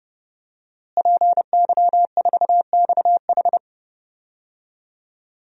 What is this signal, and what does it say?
Morse code, heard as a keyed tone, about 32 wpm, 700 Hz. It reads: PY4XH
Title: DXCC-100 Call Signs - 5 Characters - Repeated 3x - 30wpm